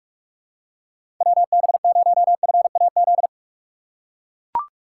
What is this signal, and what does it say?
Morse code, heard as a keyed tone, about 43 wpm, 700 Hz. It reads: WB0FAZ
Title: USA Call Signs - 3x Character Spacing - 45wpm